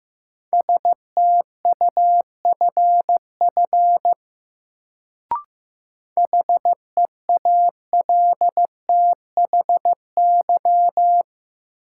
Morse code, heard as a keyed tone, about 15 wpm, 700 Hz. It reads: STUFF HEALTHY
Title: Sentences from Top 1400 Words - New Word Drill Alt - 15wpm